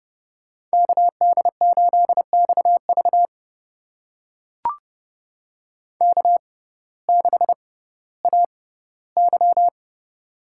Morse code, heard as a keyed tone, about 30 wpm, spaced as for 16 wpm, 700 Hz. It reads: KD8XV K6AY
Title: USA Call Signs - 6x Character Spacing - 30wpm